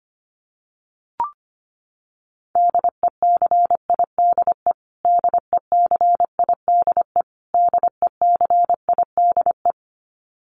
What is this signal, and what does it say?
Morse code, heard as a keyed tone, about 25 wpm, 700 Hz. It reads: DECIDE DECIDE DECIDE
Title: Top 500 Words Repeated 25wpm